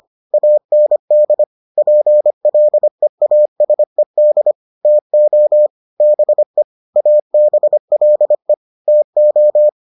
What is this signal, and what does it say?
Morse code, heard as a keyed tone, about 25 wpm, 600 Hz. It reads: AND PLEASED TO BE ABLE TO